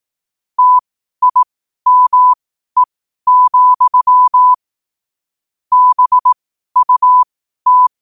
Morse code, heard as a keyed tone, about 18 wpm, spaced as for 12 wpm, 1000 Hz. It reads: TIME, BUT
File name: qotdfiles / 12wpm